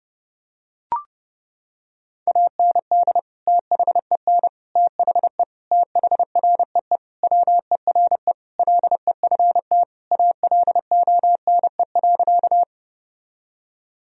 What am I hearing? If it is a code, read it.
Morse code, heard as a keyed tone, about 30 wpm, 700 Hz. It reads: AND THEN THE THREE WERE LEFT ALONE.
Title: Sentences from Top 1000 Words - Review Part 1 - 30wpm